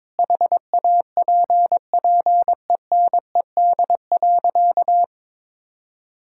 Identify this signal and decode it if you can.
Morse code, heard as a keyed tone, about 22 wpm, 700 Hz. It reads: HAPPENED.
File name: practice / 22wpm / Sentences with 8-Word Phrases - 22wpm